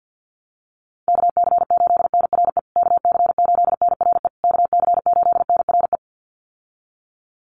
Morse code, heard as a keyed tone, about 50 wpm, 700 Hz. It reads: KC8NRE KC8NRE KC8NRE
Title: USA Call Signs Repeated 3x 50wpm